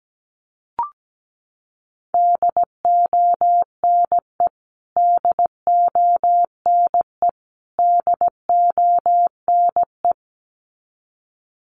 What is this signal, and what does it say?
Morse code, heard as a keyed tone, about 17 wpm, 700 Hz. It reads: DONE DONE DONE
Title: Top 500 Words Repeated 17wpm